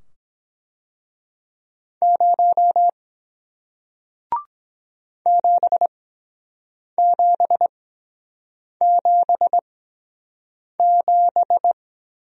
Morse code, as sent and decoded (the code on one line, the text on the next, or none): ----- / --... --... --... --...
0 7777